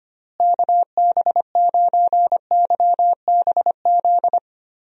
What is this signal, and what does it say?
Morse code, heard as a keyed tone, about 25 wpm, 700 Hz. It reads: KB9YBZ
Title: USA Call Signs - 6 Characters - Repeated 3x - 25wpm